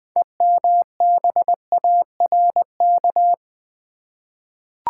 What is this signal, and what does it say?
Morse code, heard as a keyed tone, about 20 wpm, 700 Hz. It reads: EMBARK
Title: 6 Letter Words - First 3 Letters Given - 20wpm